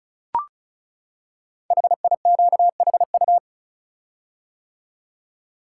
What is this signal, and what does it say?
Morse code, heard as a keyed tone, about 36 wpm, 700 Hz. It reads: HIQHU
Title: Code Groups - 5 Letters - 35wpm